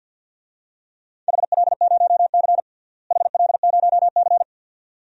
Morse code, heard as a keyed tone, about 51 wpm, 700 Hz. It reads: HL0C HL0C
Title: DXCC-100 Call Signs - 4 Characters - Repeated 2x - 50wpm